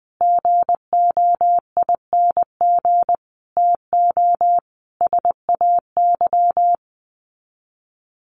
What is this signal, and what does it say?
Morse code, heard as a keyed tone, about 20 wpm, 700 Hz. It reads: GOING TO SAY
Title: Sets of 5 Words 20wpm